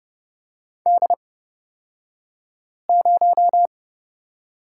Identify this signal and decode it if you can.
Morse code, heard as a keyed tone, about 30 wpm, spaced as for 5 wpm, 700 Hz. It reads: D0
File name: Lesson 38d - Letter q - Two Characters 30wpm